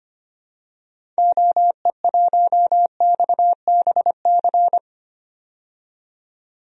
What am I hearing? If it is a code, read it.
Morse code, heard as a keyed tone, about 25 wpm, 700 Hz. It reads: OE1XBC
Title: DXCC-100 Call Signs - 6 Characters - 25wpm